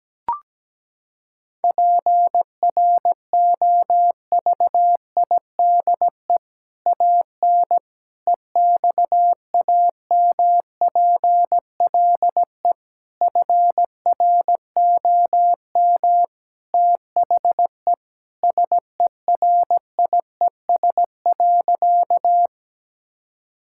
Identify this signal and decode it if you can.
Morse code, heard as a keyed tone, about 17 wpm, 700 Hz. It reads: PROVIDE AN EXAMPLE FROM THE SERIES.
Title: Sentences from Top 1000 Words - Review Part 5 - 17wpm